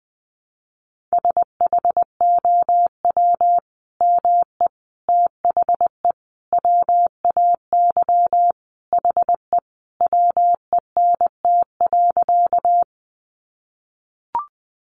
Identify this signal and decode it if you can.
Morse code, heard as a keyed tone, about 20 wpm, 700 Hz. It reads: SHOW ME THE WAY HE WENT.
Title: Sentences from Top 200 Words 20wpm